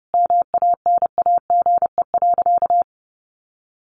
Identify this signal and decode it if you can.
Morse code, heard as a keyed tone, about 30 wpm, 700 Hz. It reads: MANAGE.
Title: Sentences from Top 1400 Words 30wpm